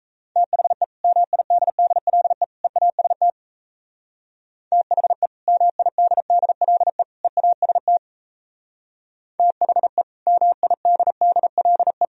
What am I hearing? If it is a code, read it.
Morse code, heard as a keyed tone, about 37 wpm, 700 Hz. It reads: THE MIDDLE EAST THE MIDDLE EAST THE MIDDLE